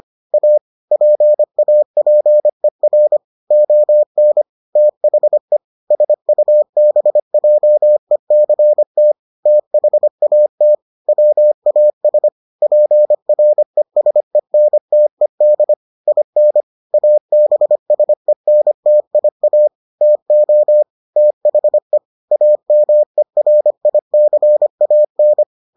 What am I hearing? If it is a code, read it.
Morse code, heard as a keyed tone, about 25 wpm, 600 Hz. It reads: A PAPER ON THE SUBJECT THAT WAS PRESENTED IN ABSENTIA TO THE AMERICAN